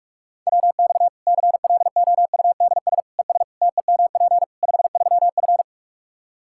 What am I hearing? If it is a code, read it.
Morse code, heard as a keyed tone, about 46 wpm, 700 Hz. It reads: WX CLOUDS ES TEMP 53F
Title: Practice QSO - Exchange 2 - Info - 45wpm